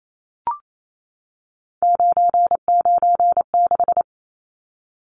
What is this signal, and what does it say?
Morse code, heard as a keyed tone, about 28 wpm, 700 Hz. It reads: 996
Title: Numbers - 3-Digit 28wpm